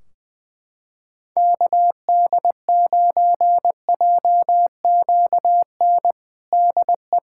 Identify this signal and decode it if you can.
Morse code, heard as a keyed tone, about 20 wpm, 700 Hz. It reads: KD9JQN DE
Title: Practice QSO - Exchange 4 - Ending - 20wpm